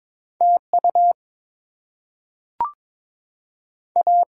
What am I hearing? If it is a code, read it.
Morse code, heard as a keyed tone, about 23 wpm, 690 Hz. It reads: TU A
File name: Top 90 QSO Elements - 8x Character Spacing - 22wpm